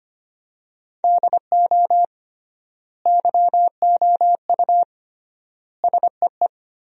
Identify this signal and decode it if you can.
Morse code, heard as a keyed tone, about 25 wpm, 700 Hz. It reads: DO YOU SEE
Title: 3x Word-Spacing - Sentences from Top 500 Words - 25wpm